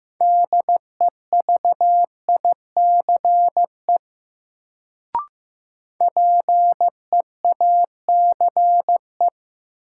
Morse code, heard as a keyed tone, about 15 wpm, 700 Hz. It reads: DEVICE PEACE
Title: Sentences from Top 1500 Words - New Word Drill Alt - 15wpm